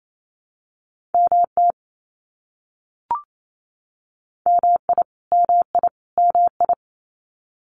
Morse code, heard as a keyed tone, about 28 wpm, 700 Hz. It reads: MT MI MI MI
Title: US States Repeated 3x 28wpm